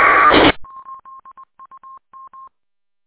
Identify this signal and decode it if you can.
There is no code.